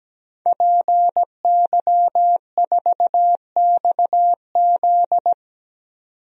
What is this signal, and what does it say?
Morse code, heard as a keyed tone, about 17 wpm, 700 Hz. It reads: PY4XZ
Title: DXCC-100 Call Signs - 5 Characters - 17wpm